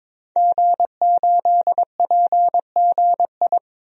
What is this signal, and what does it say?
Morse code, heard as a keyed tone, about 21 wpm, 700 Hz. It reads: G8PGI
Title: DXCC-100 Call Signs - 5 Characters - 22wpm